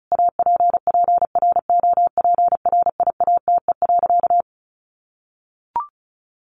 Morse code, heard as a keyed tone, about 34 wpm, 700 Hz. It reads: APPROPRIATE.
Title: Sentences from Top 900 Words 35wpm